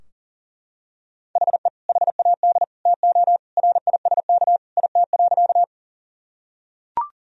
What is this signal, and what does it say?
Morse code, heard as a keyed tone, about 40 wpm, 700 Hz. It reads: HE HAD TO RISK IT.
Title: Sentences from Top 1000 Words - Review Part 4 - 40wpm